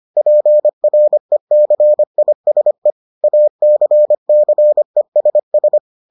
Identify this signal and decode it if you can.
Morse code, heard as a keyed tone, about 25 wpm, 600 Hz. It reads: PRECISE ACCESS